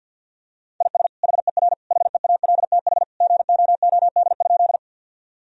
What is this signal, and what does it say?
Morse code, heard as a keyed tone, about 50 wpm, 700 Hz. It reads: IS HER HEALTH GOOD?
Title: Sentences from Top 1000 Words - Review Part 3 - 50wpm